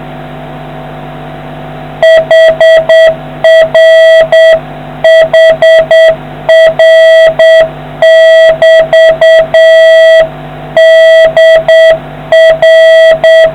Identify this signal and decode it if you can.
Morse code, heard as a keyed tone, about 8 wpm, 650 Hz, with noise. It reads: HR HR<BT>DR